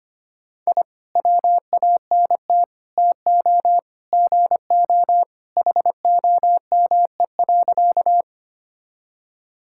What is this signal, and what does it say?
Morse code, heard as a keyed tone, about 25 wpm, 700 Hz. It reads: I WANT TO GO HOME.
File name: practice / 25wpm / Sentences with 3-Word Phrases - 25wpm